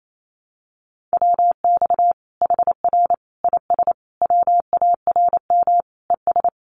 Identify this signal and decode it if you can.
Morse code, heard as a keyed tone, about 28 wpm, 700 Hz. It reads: WX HR IS WARM ES
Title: Common QSO Phrases 28wpm